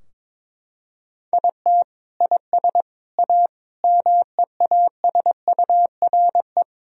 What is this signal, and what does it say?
Morse code, heard as a keyed tone, about 21 wpm, 700 Hz. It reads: IT IS A MEASURE
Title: Sentences from Top 1000 Words - Review Part 2 - 22wpm